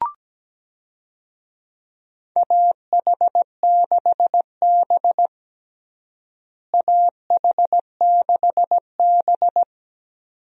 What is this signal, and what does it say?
Morse code, heard as a keyed tone, about 17 wpm, 700 Hz. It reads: AH6B AH6B
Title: USA Call Signs - 4 Characters - Repeated 2x - 17wpm